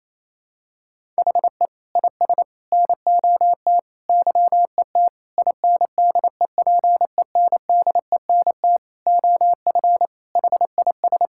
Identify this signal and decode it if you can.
Morse code, heard as a keyed tone, about 27 wpm, 700 Hz. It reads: HE IS NOT YET INDEPENDENT OF HIS
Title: Sentences from Top 1000 Words - Review Part 6 - 28wpm